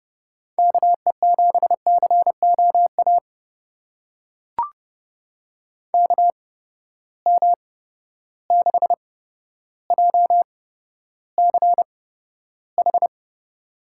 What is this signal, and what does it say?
Morse code, heard as a keyed tone, about 30 wpm, spaced as for 13 wpm, 700 Hz. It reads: KE7COA KM6JCH
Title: USA Call Signs - 6 Characters - 8x Character Spacing - 30wpm